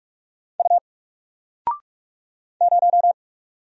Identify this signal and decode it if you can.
Morse code, heard as a keyed tone, about 45 wpm, 700 Hz. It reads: U 0
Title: Single Letter-Number 45wpm